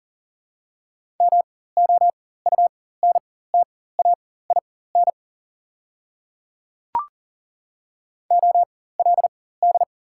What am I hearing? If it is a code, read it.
Morse code, heard as a keyed tone, about 40 wpm, spaced as for 19 wpm, 700 Hz. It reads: MOUNTAIN OLD
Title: 4x Character Spacing - Top 300 Words - 40wpm